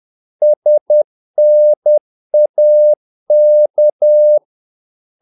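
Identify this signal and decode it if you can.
Morse code, heard as a keyed tone, about 10 wpm, 600 Hz. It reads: SNAK
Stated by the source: Morse practice of short Danish words at 10 wpm, from CW med Gnister.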